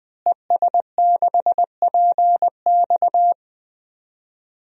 Morse code, heard as a keyed tone, about 20 wpm, 700 Hz. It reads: ES6PX
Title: DXCC-100 Call Signs - 5 Characters - Repeated 3x - 20wpm